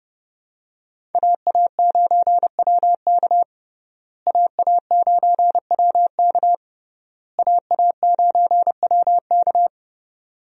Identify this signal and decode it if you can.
Morse code, heard as a keyed tone, about 30 wpm, 700 Hz. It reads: AA9WK AA9WK AA9WK
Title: USA Call Signs - 5 Characters - Repeated 3x - 30wpm